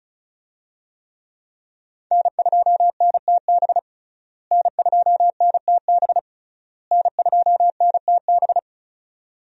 Morse code, heard as a keyed tone, about 36 wpm, 700 Hz. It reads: N2NTB N2NTB N2NTB
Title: USA Call Signs - 5 Characters - Repeated 3x - 35wpm